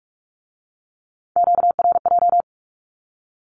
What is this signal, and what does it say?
Morse code, heard as a keyed tone, about 46 wpm, 700 Hz. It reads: QRJ
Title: Q-Codes with Spoken Meaning - 45wpm